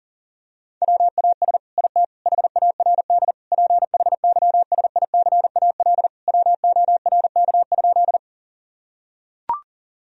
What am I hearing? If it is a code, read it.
Morse code, heard as a keyed tone, about 40 wpm, 700 Hz. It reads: WAS IT HARD PHYSICAL WORK?
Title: Sentences from Top 1000 Words - Review Part 4 - 40wpm